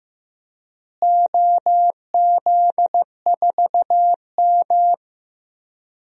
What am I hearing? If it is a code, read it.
Morse code, heard as a keyed tone, about 15 wpm, 700 Hz. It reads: OZ4M
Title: DXCC-100 Call Signs - 4 Characters - Repeated 2x - 15wpm